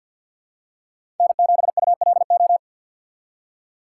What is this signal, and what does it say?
Morse code, heard as a keyed tone, about 49 wpm, 700 Hz. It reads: N7ULO
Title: USA Call Signs - 5 Characters - 4x Character Spacing - 50wpm